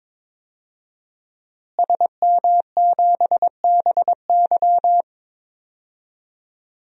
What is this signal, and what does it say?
Morse code, heard as a keyed tone, about 22 wpm, 700 Hz. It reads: SM7BY
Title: DXCC-100 Call Signs - 5 Characters - Repeated 2x - 22wpm